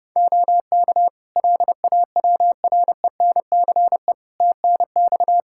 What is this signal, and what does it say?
Morse code, heard as a keyed tone, about 30 wpm, 700 Hz. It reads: OK LAWRENCE TNX
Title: Practice QSO - Exchange 4 - Ending - 30wpm